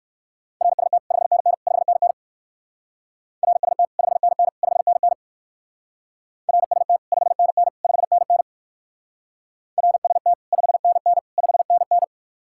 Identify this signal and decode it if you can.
Morse code, heard as a keyed tone, about 53 wpm, 700 Hz. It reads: RST 5NN 5NN RST 5NN 5NN RST 5NN 5NN RST 5NN 5NN
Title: Speed-Racing Common QSO Phrases Encore - 45wpm